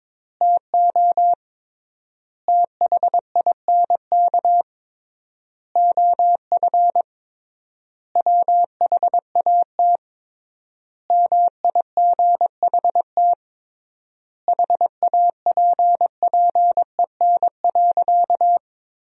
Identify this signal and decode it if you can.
Morse code, heard as a keyed tone, about 22 wpm, 700 Hz. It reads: TO THINK OF WHAT MIGHT HAPPEN.
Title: 3x Word-Spacing - Sentences from Top 400 Words - 22wpm